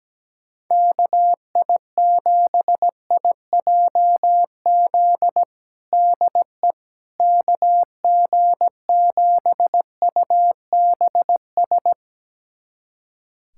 Morse code, heard as a keyed tone, about 17 wpm, 700 Hz. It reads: KI7IJZ DE KG7UBS
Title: USA Callsign DE Callsign 17wpm